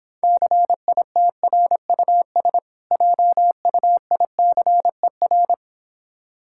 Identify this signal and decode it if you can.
Morse code, heard as a keyed tone, about 26 wpm, 700 Hz. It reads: CITRUS JUICER